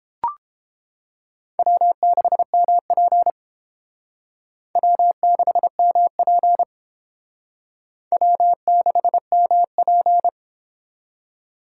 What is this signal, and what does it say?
Morse code, heard as a keyed tone, about 30 wpm, 700 Hz. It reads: W6MP W6MP W6MP